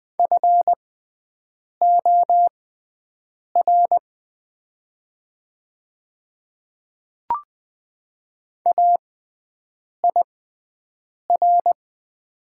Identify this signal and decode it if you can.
Morse code, heard as a keyed tone, about 20 wpm, spaced as for 7 wpm, 700 Hz. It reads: FOR AIR
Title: 6x Character Spacing - Top 300 Words - 20wpm